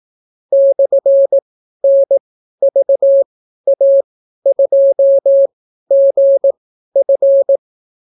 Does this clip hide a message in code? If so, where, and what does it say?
Yes: Morse code from 0.5 to 7.6 s, heard as a keyed tone, about 18 wpm, spaced as for 12 wpm, 550 Hz. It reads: /NVA2GF